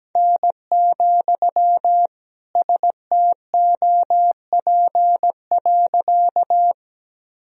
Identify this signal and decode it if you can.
Morse code, heard as a keyed tone, about 17 wpm, 700 Hz. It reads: N, STOP.
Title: Sentences from Top 300 Words and Abbreviations 17wpm